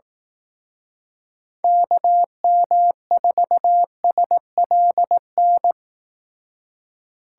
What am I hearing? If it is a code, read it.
Morse code, heard as a keyed tone, about 19 wpm, 700 Hz. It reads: KM4SLN
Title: Speed Racing - USA 2x3 Call Signs 15wpm